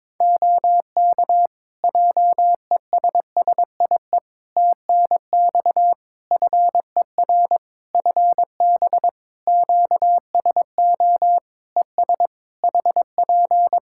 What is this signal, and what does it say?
Morse code, heard as a keyed tone, about 22 wpm, 700 Hz. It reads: OK JESSIE TNX FER FB QSO ES HP
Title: Practice QSO - Exchange 4 - Ending - 22wpm